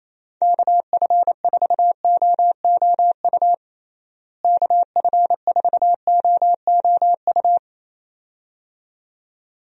Morse code, heard as a keyed tone, about 28 wpm, 700 Hz. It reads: KF4OOU KF4OOU
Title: USA Call Signs - 6 Characters - Repeated 2x - 28wpm